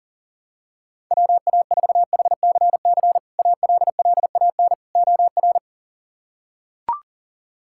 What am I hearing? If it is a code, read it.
Morse code, heard as a keyed tone, about 40 wpm, 700 Hz. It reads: WA4HCC ALLAN OR
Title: K1USN SST QSO Practice - 40wpm